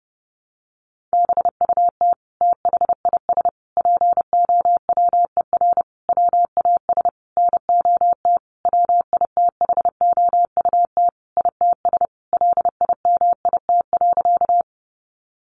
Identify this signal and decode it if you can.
Morse code, heard as a keyed tone, about 30 wpm, 700 Hz. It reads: BUT THIS POWER WAS NOT WITHOUT ITS LIMIT.
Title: Sentences from Top 1500 Words 30wpm